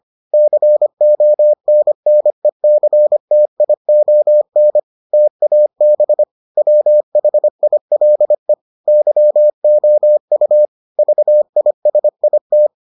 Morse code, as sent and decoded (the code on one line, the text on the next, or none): -.-. --- -. -. -.-. - .. --- -. / - .- -... / .-- .... .. .-.. / -.-- --- ..- / ...- .. ... .. -
CONNECTION TAB WHILE YOU VISIT